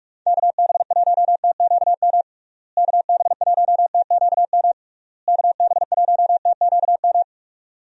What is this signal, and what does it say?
Morse code, heard as a keyed tone, about 42 wpm, 700 Hz. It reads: KB1TQM KB1TQM KB1TQM
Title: USA Call Signs - 6 Characters - Repeated 3x - 45wpm